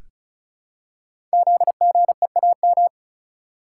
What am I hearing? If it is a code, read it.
Morse code, heard as a keyed tone, about 36 wpm, 700 Hz. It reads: ZGEAM
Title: Code Groups - 5 Letters - 35wpm